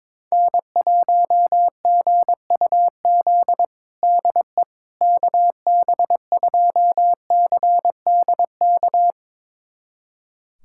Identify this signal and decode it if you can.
Morse code, heard as a keyed tone, about 22 wpm, 700 Hz. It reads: N1GUZ DE KB2CDK
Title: USA Callsign DE Callsign 22wpm